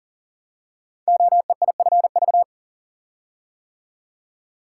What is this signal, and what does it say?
Morse code, heard as a keyed tone, about 40 wpm, 690 Hz. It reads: OEIFV
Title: Code Groups - 5 Letters - 40wpm